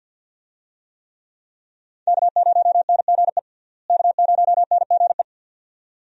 Morse code, heard as a keyed tone, about 48 wpm, 700 Hz. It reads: K0NGE K0NGE
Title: USA Call Signs - 5 Characters - Repeated 2x - 50wpm